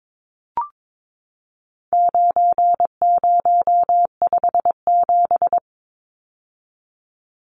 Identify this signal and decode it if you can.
Morse code, heard as a keyed tone, about 22 wpm, 700 Hz. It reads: 9057
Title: Numbers - 4-Digit 22wpm